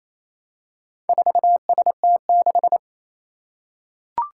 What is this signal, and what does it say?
Morse code, heard as a keyed tone, about 27 wpm, 700 Hz. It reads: 4ST6
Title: Four Challenging Letter-Number Combinations 28wpm